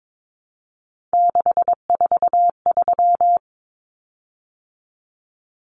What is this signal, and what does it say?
Morse code, heard as a keyed tone, about 22 wpm, 700 Hz. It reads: 643
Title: Common Cut Numbers - 3-Digit 22wpm